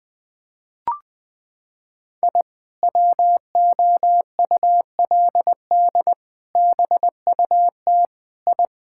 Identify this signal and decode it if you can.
Morse code, heard as a keyed tone, about 21 wpm, 700 Hz. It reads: I WOULD BUT I
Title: Sentences with 2-Word Phrases - 20wpm